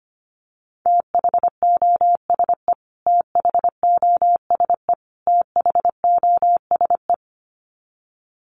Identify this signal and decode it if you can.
Morse code, heard as a keyed tone, about 25 wpm, 700 Hz. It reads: THOSE THOSE THOSE
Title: Top 500 Words Repeated 25wpm